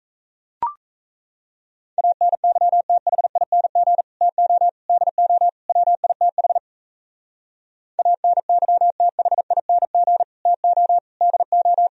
Morse code, heard as a keyed tone, about 38 wpm, 700 Hz. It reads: ANYTHING TO DO WITH ANYTHING TO DO